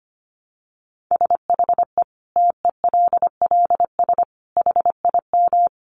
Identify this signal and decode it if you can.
Morse code, heard as a keyed tone, about 25 wpm, 700 Hz. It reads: SHE TELLS HIM
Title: Additional Easy Phrases 25wpm